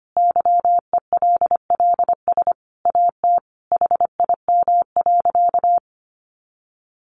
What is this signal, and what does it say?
Morse code, heard as a keyed tone, about 25 wpm, 700 Hz. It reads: YELLS AT HIM.
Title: Additional Easy Phrases 25wpm